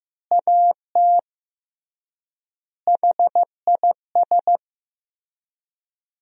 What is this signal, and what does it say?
Morse code, heard as a keyed tone, about 16 wpm, 700 Hz. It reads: AT HIS
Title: 3x Word-Spacing - Sentences from Top 300 Words - 15wpm